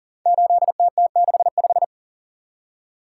Morse code, heard as a keyed tone, about 39 wpm, 690 Hz. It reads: 8TT65
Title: Code Groups - 5 Letter:Numbers - 40wpm